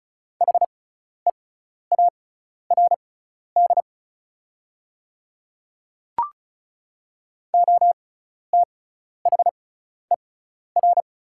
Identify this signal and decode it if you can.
Morse code, heard as a keyed tone, about 34 wpm, spaced as for 12 wpm, 690 Hz. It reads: HEARD OTHER
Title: 6x Character Spacing - Top 500 Words - 35wpm